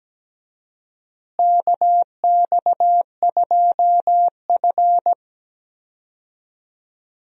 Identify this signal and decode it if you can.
Morse code, heard as a keyed tone, about 17 wpm, 700 Hz. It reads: KX2F
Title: USA Call Signs - 4 Characters - Repeated 2x - 17wpm